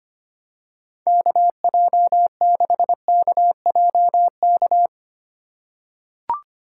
Morse code, heard as a keyed tone, about 25 wpm, 700 Hz. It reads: KJ6KJK
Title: USA Call Signs - 2x Character Spacing - 25wpm